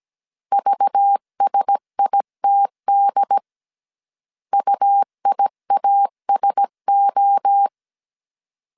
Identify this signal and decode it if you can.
Morse code, heard as a keyed tone, about 17 wpm, 800 Hz. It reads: VSITD UIASO
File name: CW / DET / 1 6 1500C S C 16wpm 2733